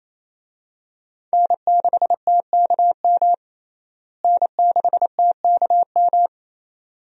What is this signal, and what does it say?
Morse code, heard as a keyed tone, about 28 wpm, 700 Hz. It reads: N6TKM N6TKM
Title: USA Call Signs - 5 Characters - Repeated 2x - 28wpm